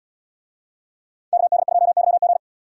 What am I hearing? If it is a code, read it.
Morse code, heard as a keyed tone, about 77 wpm, 700 Hz. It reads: KD2YG